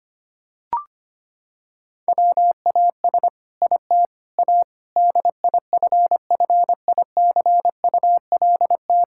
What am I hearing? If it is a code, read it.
Morse code, heard as a keyed tone, about 25 wpm, 700 Hz. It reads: WAS IT A DIFFICULT
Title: Sentences from Top 1000 Words - Review Part 6 - 25wpm